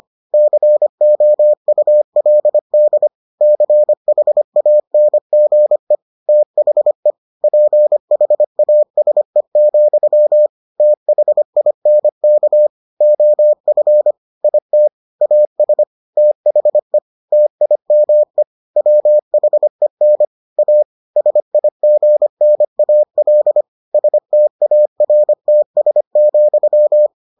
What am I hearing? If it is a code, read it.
Morse code, heard as a keyed tone, about 25 wpm, 600 Hz. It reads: COULD CHANGE THE PHASE, THINK OF IT AS THE TIME WHEN A SIGNAL STARTS,